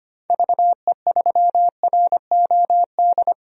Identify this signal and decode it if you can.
Morse code, heard as a keyed tone, about 25 wpm, 700 Hz. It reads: VE3ROD
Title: DXCC-100 Call Signs - 6 Characters - Repeated 3x - 25wpm